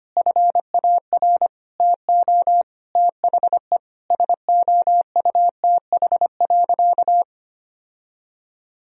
Morse code, heard as a keyed tone, about 25 wpm, 700 Hz. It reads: FAR TO THE SOUTH.
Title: Sentences from Top 1000 Words - Review Part 2 - 25wpm